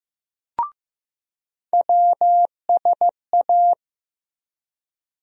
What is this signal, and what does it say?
Morse code, heard as a keyed tone, about 15 wpm, 700 Hz. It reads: WSA
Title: USA Call Sign Postfixes - 3 Letters 15wpm